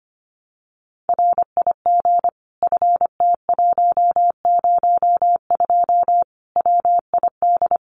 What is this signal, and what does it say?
Morse code, heard as a keyed tone, about 25 wpm, 700 Hz. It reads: RIG FT102 WID
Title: Common QSO Phrases 25wpm